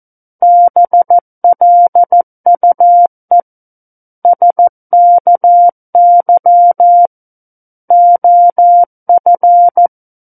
Morse code, heard as a keyed tone, about 14 wpm, 700 Hz. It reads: BLUE SKY OF
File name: Book One / 05-Chapter 5 - 14 WPM